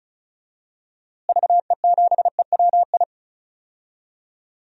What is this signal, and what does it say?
Morse code, heard as a keyed tone, about 36 wpm, 700 Hz. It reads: VE7EWI
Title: DXCC-100 Call Signs - 6 Characters - Repeated 2x - 35wpm